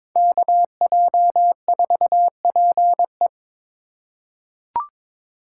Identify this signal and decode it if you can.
Morse code, heard as a keyed tone, about 22 wpm, 700 Hz. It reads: KJ4PE
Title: USA Call Signs - 5 Characters - 2x Character Spacing - 22wpm